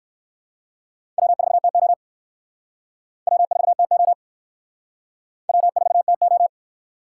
Morse code, heard as a keyed tone, about 66 wpm, 700 Hz. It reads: W4TO W4TO W4TO